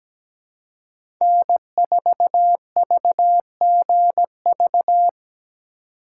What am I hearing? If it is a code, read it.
Morse code, heard as a keyed tone, about 17 wpm, 700 Hz. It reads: N4VGV
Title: USA Call Signs - 5 Characters - 2x Character Spacing - 17wpm